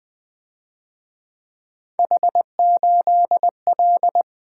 Morse code, heard as a keyed tone, about 24 wpm, 700 Hz. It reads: H8L
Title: Speed-Racing - Three Letter-Number Combinations - 20wpm